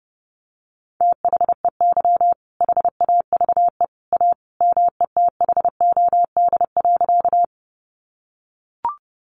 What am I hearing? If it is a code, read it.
Morse code, heard as a keyed tone, about 30 wpm, 700 Hz. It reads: THEY HAVE A METHOD.
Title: Sentences from Top 600 Words 30wpm